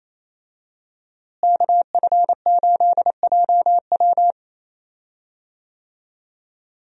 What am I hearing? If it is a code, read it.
Morse code, heard as a keyed tone, about 28 wpm, 700 Hz. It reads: KF8JW
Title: USA Call Signs - 5 Characters - Repeated 2x - 28wpm